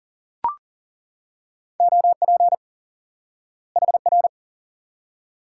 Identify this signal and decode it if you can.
Morse code, heard as a keyed tone, about 40 wpm, 700 Hz. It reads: OP HR
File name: IQE Course 3 - Lesson 89 - YAGI - Two Elements - 40wpm